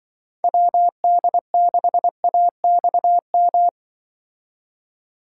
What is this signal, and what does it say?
Morse code, heard as a keyed tone, about 23 wpm, 700 Hz. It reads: WD6AXM
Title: Speed Racing - USA 2x3 Call Signs 20wpm